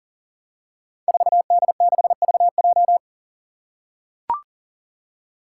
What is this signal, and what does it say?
Morse code, heard as a keyed tone, about 38 wpm, 700 Hz. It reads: 4D6VJ
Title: Five Challenging Letter-Number Combinations 40wpm